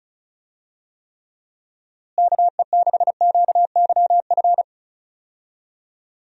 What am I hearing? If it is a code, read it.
Morse code, heard as a keyed tone, about 36 wpm, 700 Hz. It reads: KE6QYF
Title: USA Call Signs - 6 Characters - Repeated 3x - 35wpm